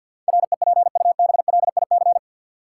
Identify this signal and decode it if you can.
Morse code, heard as a keyed tone, about 53 wpm, 700 Hz. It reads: REPUBLIC
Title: Sentences from Top 1500 Words - New Word Drill Alt - 50wpm